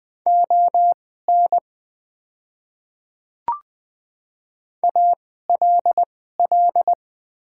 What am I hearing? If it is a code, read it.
Morse code, heard as a keyed tone, about 21 wpm, spaced as for 15 wpm, 700 Hz. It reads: ON ALL
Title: 2x Character Spacing - Top 100 Words - 20wpm